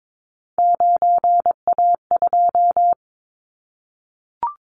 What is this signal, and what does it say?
Morse code, heard as a keyed tone, about 22 wpm, 700 Hz. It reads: 9A2
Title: Callsign Prefixes for Top 25 DXCC - Repeated 3x - 22wpm